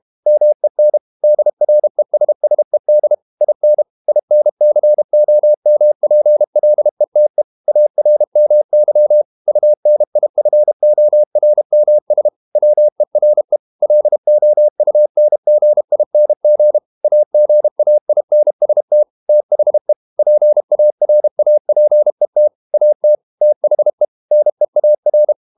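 Morse code, heard as a keyed tone, about 32 wpm, 600 Hz. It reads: MEN DRESSED IN INCOMPLETE ARMY UNIFORMS WERE LOUNGING AGAINST THE PARAPET AT THE NEAR
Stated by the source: Morse code practice at 32 wpm from CW med Gnister, reading Ken Follett's novel Never in English.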